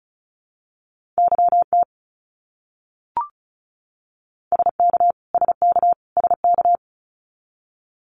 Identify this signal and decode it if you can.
Morse code, heard as a keyed tone, about 34 wpm, 700 Hz. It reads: YT SK SK SK
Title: Canadian Provinces and Territories - Repeated 3x - 35wpm